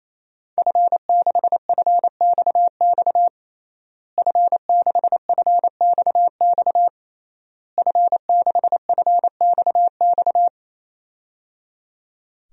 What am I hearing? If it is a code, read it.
Morse code, heard as a keyed tone, about 28 wpm, 700 Hz. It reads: F6FXX F6FXX F6FXX
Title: DXCC-100 Call Signs - 5 Characters - Repeated 3x - 28wpm